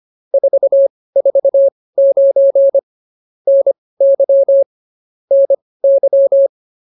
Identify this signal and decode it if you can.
Morse code, heard as a keyed tone, about 24 wpm, spaced as for 18 wpm, 550 Hz. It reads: 449 NY NY
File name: POTA206 18